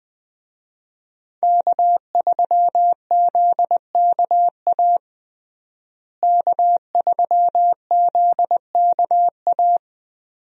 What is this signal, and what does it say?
Morse code, heard as a keyed tone, about 21 wpm, 700 Hz. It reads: K3ZKA K3ZKA
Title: USA Call Signs - 5 Characters - Repeated 2x - 20wpm